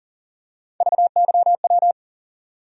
Morse code, heard as a keyed tone, about 40 wpm, 700 Hz. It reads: VYW
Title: USA Call Sign Postfixes - 3 Letters 40wpm